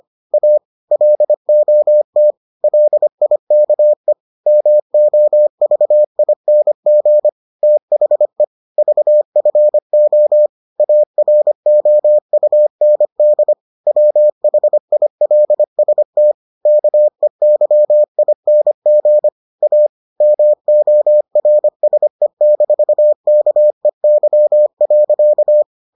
Morse code, heard as a keyed tone, about 25 wpm, 600 Hz. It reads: A LOT LIKE MOVING THE VFO AROUND WHILST KEYING A MORSE-KEY.